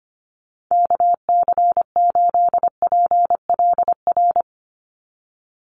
Morse code, heard as a keyed tone, about 25 wpm, 700 Hz. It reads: KC8PLR
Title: USA Call Signs 25wpm